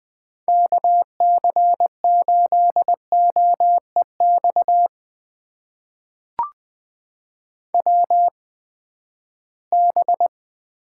Morse code, heard as a keyed tone, about 20 wpm, spaced as for 14 wpm, 700 Hz. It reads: KC8OEX WB
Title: USA Call Signs - 6 Characters - 8x Character Spacing - 20wpm